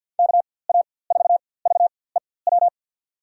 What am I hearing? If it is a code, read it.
Morse code, heard as a keyed tone, about 48 wpm, spaced as for 23 wpm, 700 Hz. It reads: KA4VEW
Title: USA Call Signs - 6 Characters - 4x Character Spacing - 50wpm